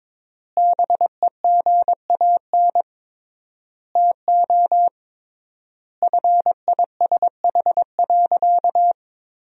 Morse code, heard as a keyed tone, about 22 wpm, 700 Hz. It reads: BEGAN TO FISH.
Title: 3x Word-Spacing - Sentences from Top 300 Words - 22wpm